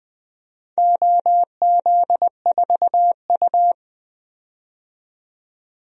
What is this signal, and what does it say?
Morse code, heard as a keyed tone, about 20 wpm, 700 Hz. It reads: OZ4U
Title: DXCC-100 Call Signs - 4 Characters - 20wpm